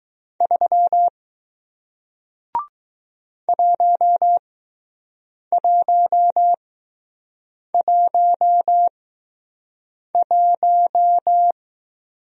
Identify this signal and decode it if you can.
Morse code, heard as a keyed tone, about 19 wpm, spaced as for 6 wpm, 700 Hz. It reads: 3 1111